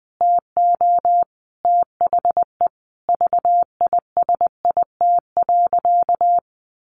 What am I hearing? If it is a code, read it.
Morse code, heard as a keyed tone, about 19 wpm, 700 Hz. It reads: TO THE VISIT.
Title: Sentences from Top 1200 Words 20wpm